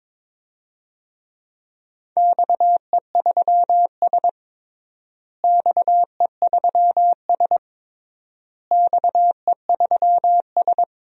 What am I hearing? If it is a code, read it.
Morse code, heard as a keyed tone, about 22 wpm, 700 Hz. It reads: XE3S XE3S XE3S
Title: DXCC-100 Call Signs - 4 Characters - Repeated 3x - 22wpm